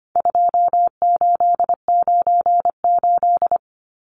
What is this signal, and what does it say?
Morse code, heard as a keyed tone, about 25 wpm, 700 Hz. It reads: 2898
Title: Numbers - 4-Digit 25wpm